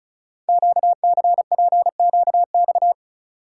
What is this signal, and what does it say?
Morse code, heard as a keyed tone, about 36 wpm, 700 Hz. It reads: QCPQX
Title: Code Groups - 5 Letters - 35wpm